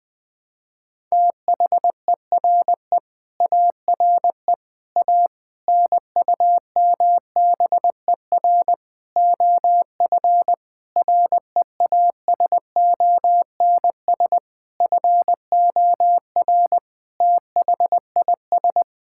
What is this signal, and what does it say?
Morse code, heard as a keyed tone, about 20 wpm, 700 Hz. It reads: THERE ARE A NUMBER OF REASONS FOR THIS
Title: Sets of 8 Words - 20wpm